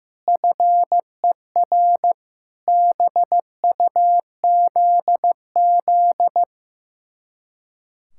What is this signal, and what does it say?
Morse code, heard as a keyed tone, about 15 wpm, 700 Hz. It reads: FER BUZZ
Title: Speed-Racing Common QSO Phrases Encore - 15wpm